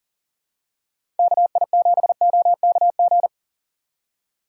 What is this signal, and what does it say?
Morse code, heard as a keyed tone, about 40 wpm, 700 Hz. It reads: KI7OKG
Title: USA Call Signs - 6 Characters - 3x Character Spacing - 40wpm